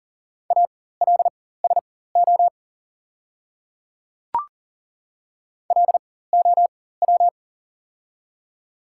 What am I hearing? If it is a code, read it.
Morse code, heard as a keyed tone, about 40 wpm, spaced as for 19 wpm, 700 Hz. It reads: ALSO LOW
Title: 4x Character Spacing - Top 300 Words - 40wpm